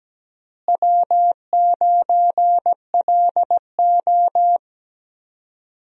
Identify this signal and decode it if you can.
Morse code, heard as a keyed tone, about 17 wpm, 700 Hz. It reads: W9LO
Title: USA Call Signs - 4 Characters - Repeated 2x - 17wpm